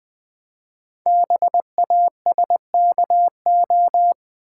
Morse code, heard as a keyed tone, about 20 wpm, 700 Hz. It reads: BASKO
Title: Code Groups - 5 Letters - 20wpm